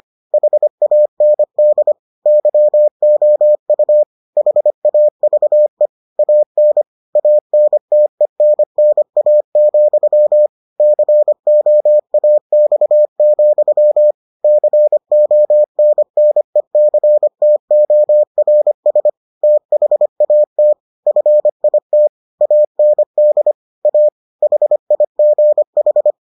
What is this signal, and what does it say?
Morse code, heard as a keyed tone, about 25 wpm, 600 Hz. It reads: HAND YOU HAVE AN ANTENNA, COAX, CONNECTORS THAT FIT AND A HIGH